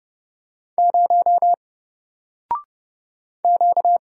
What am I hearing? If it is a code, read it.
Morse code, heard as a keyed tone, about 29 wpm, 700 Hz. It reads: 0 Q
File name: Single-Letter-Number-30wpm